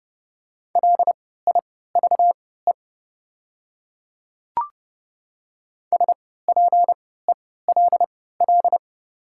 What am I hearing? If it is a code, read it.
Morse code, heard as a keyed tone, about 30 wpm, spaced as for 17 wpm, 700 Hz. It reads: LIVE SPELL
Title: 3x Character Spacing - Top 200 Words - 30wpm